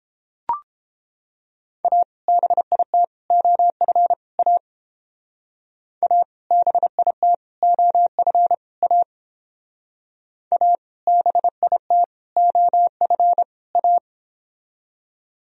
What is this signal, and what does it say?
Morse code, heard as a keyed tone, about 31 wpm, 700 Hz. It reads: A BIT OF A A BIT OF A A BIT OF A